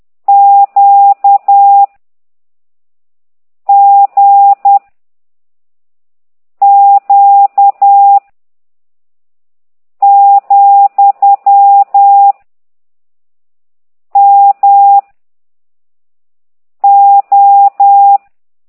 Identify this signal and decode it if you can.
Morse code, heard as a keyed tone, about 10 wpm, 800 Hz. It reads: Q G Q , M O